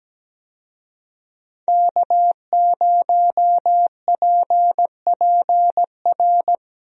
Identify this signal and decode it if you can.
Morse code, heard as a keyed tone, about 17 wpm, 700 Hz. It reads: K0PPR
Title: USA Call Signs - 5 Characters - Repeated 3x - 17wpm